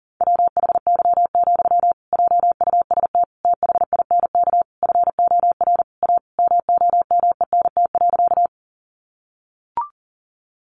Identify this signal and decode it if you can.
Morse code, heard as a keyed tone, about 40 wpm, 700 Hz. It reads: WHY, JUST THINK FOR A MOMENT.
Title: Sentences from Top 1000 Words 40wpm